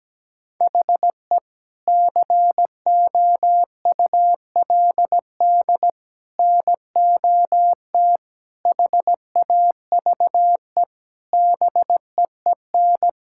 Morse code, heard as a keyed tone, about 17 wpm, 700 Hz. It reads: HE COULD NOT HAVE BEEN
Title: Sentences from Top 1600 Words 17wpm